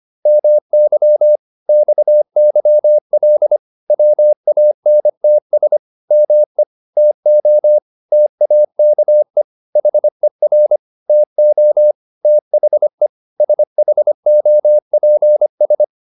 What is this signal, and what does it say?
Morse code, heard as a keyed tone, about 25 wpm, 600 Hz. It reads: MY XYL WANTS ME TO TAKE HER TO THE SHOPS
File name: QSO406 25